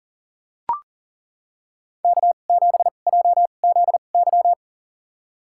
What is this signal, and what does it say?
Morse code, heard as a keyed tone, about 40 wpm, spaced as for 29 wpm, 700 Hz. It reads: K7JZY
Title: USA Call Signs - 2x Character Spacing - 40wpm